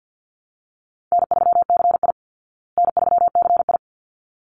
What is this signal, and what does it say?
Morse code, heard as a keyed tone, about 50 wpm, 700 Hz. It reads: N3CI N3CI
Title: Easy Call Signs Repeated 2x - 50wpm